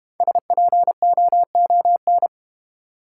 Morse code, heard as a keyed tone, about 31 wpm, 700 Hz. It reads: SPOON
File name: Easy Words - Speed-Racing with Hints - 25wpm